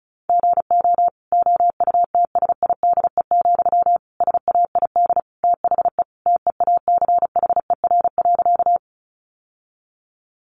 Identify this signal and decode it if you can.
Morse code, heard as a keyed tone, about 34 wpm, 700 Hz. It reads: GO OUTSIDE, SAID THE TEACHER.
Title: Sentences from Top 900 Words 35wpm